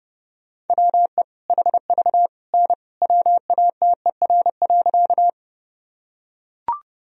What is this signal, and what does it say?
Morse code, heard as a keyed tone, about 30 wpm, 700 Hz. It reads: WE HV N WATER.
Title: Sentences from Top 100 Words and Abbreviations 30wpm